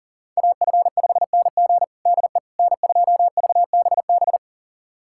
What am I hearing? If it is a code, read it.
Morse code, heard as a keyed tone, about 40 wpm, 700 Hz. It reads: AF5NG DE N2VBB
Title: USA Callsign DE Callsign - 5 Characters - 40wpm